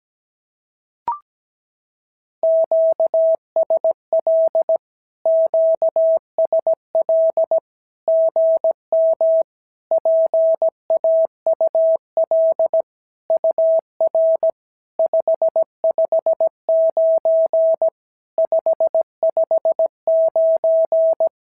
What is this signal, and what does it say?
Morse code, heard as a keyed tone, about 17 wpm, 650 Hz. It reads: QSL QSL GM PAUL UR 559 559
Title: SOTA QSO Practice 17wpm